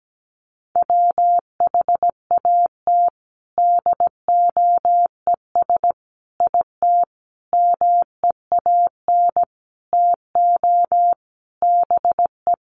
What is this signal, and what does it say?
Morse code, heard as a keyed tone, about 17 wpm, 700 Hz. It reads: WHAT DOES IT MEAN TO BE
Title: Sentences from Top 1400 Words 17wpm